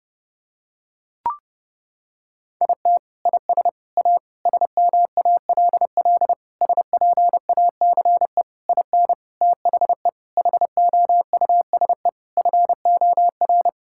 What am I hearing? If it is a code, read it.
Morse code, heard as a keyed tone, about 32 wpm, 700 Hz. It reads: IT IS A SMALL SPACE IN THE HOUSE FOR
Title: Sentences from Top 1000 Words - Review Part 5 - 30wpm